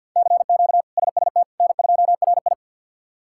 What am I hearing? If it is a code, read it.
Morse code, heard as a keyed tone, about 48 wpm, 700 Hz. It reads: CQ SST N2RI
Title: K1USN SST QSO Practice - 50wpm